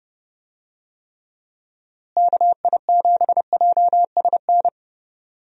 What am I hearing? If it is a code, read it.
Morse code, heard as a keyed tone, about 30 wpm, 700 Hz. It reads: KI7JSN
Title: USA Call Signs - 6 Characters - Repeated 3x - 30wpm